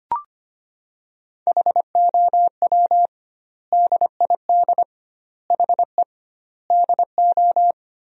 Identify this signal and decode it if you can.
Morse code, heard as a keyed tone, about 25 wpm, 700 Hz. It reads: HOW DID HE DO
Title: 2x Word-Spacing - Sentences from Top 100 Words - 25wpm